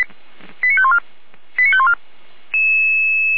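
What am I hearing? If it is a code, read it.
Five-tone selective call (ZVEI), heard as quick runs of five tones, tones of about 70 ms. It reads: 89514 89514
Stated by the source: ZVEI five-tone selective call received by SDR.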